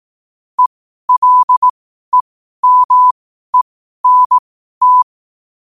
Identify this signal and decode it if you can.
Morse code, heard as a keyed tone, about 18 wpm, spaced as for 12 wpm, 1000 Hz. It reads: ELEMENT
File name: qotdfiles / 12wpm